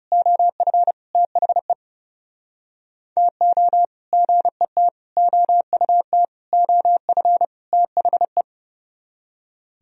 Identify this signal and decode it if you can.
Morse code, heard as a keyed tone, about 32 wpm, 700 Hz. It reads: OF THE TO GET OUT OF THE
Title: Speed-Racing - Sets of 5 Words 30wpm